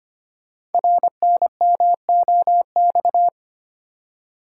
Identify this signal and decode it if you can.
Morse code, heard as a keyed tone, about 25 wpm, 700 Hz. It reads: RNMOX
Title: Code Groups - 5 Letters - 25wpm